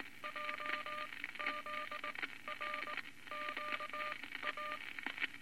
Morse code, heard as a keyed tone, about 19 wpm, spaced as for 14 wpm, 1200 Hz, with noise. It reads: JZRQA
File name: Morse Code Records / 19 B LCM 167 Twelve Words Per Minute